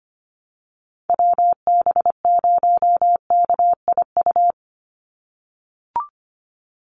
Morse code, heard as a keyed tone, about 25 wpm, 700 Hz. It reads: WB0KIU
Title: NA SOTA Chaser Callsigns 25wpm